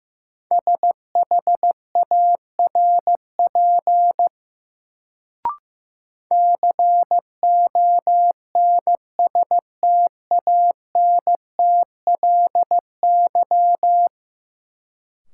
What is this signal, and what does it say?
Morse code, heard as a keyed tone, about 15 wpm, 700 Hz. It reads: SHARP CONSTANTLY
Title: Sentences from Top 1600 Words - New Word Drill Alt - 15wpm